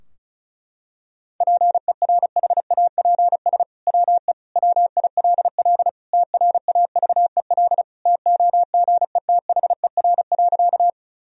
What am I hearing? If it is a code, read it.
Morse code, heard as a keyed tone, about 36 wpm, 700 Hz. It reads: PERHAPS WE WILL TRAVEL TOGETHER.
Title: Sentences from Top 1000 Words - Review Part 2 - 35wpm